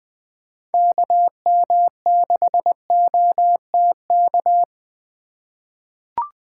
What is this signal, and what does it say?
Morse code, heard as a keyed tone, about 20 wpm, 700 Hz. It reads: KM6OTK
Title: USA Call Signs - 2x Character Spacing - 20wpm